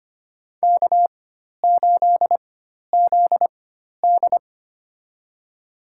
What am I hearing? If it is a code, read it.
Morse code, heard as a keyed tone, about 25 wpm, spaced as for 12 wpm, 700 Hz. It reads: K8ZD
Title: USA Call Signs - 4x Character Spacing - 25wpm